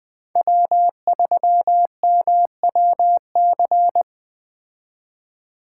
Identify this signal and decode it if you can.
Morse code, heard as a keyed tone, about 21 wpm, 700 Hz. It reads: W3MWC